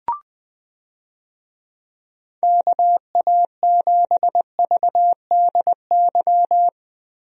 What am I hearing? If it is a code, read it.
Morse code, heard as a keyed tone, about 20 wpm, 700 Hz. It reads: KA7VDY
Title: USA Call Signs - 6 Characters - Repeated 2x - 20wpm